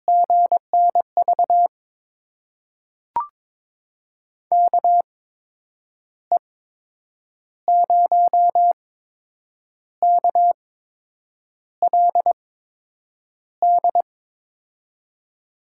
Morse code, heard as a keyed tone, about 22 wpm, spaced as for 8 wpm, 700 Hz. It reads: GNV KE0KLD
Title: USA Call Signs - 6 Characters - 8x Character Spacing - 22wpm